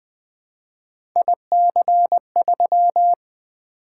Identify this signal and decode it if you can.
Morse code, heard as a keyed tone, about 20 wpm, 700 Hz. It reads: IC3
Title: Three Letter-Number Combinations - 20wpm